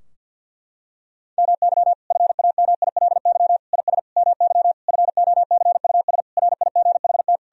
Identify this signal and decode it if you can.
Morse code, heard as a keyed tone, about 51 wpm, 700 Hz. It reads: MY FAMILY IS MY FOCUS RIGHT
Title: Sentences from Top 1000 Words - Review Part 4 - 50wpm